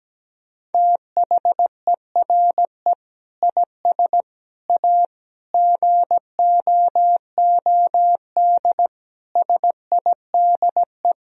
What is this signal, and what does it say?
Morse code, heard as a keyed tone, about 17 wpm, 700 Hz. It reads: THERE IS A GOOD SIDE
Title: Sentences from Top 1000 Words - Review Part 1 - 17wpm